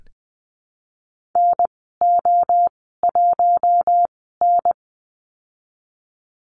Morse code, heard as a keyed tone, about 19 wpm, spaced as for 14 wpm, 700 Hz. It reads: NO1N
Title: 055 Lesson 10g - Letter r - Call Signs 20wpm